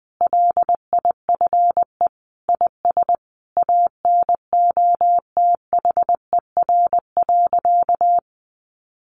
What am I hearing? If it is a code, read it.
Morse code, heard as a keyed tone, about 20 wpm, 700 Hz. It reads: LIFE IS ANOTHER.
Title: Sentences from Top 1000 Words 20wpm